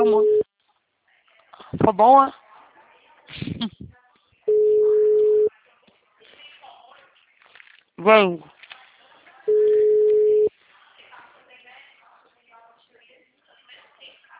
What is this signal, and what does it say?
Ringback tone (Europe, 425 Hz), with speech.
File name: out- -2022- - - .45784